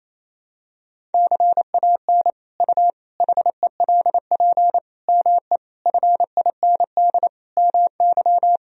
Morse code, heard as a keyed tone, about 28 wpm, 700 Hz. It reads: CAN U HELP ME FIND MY
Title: Sentences from Top 200 Words and Abbreviations 28wpm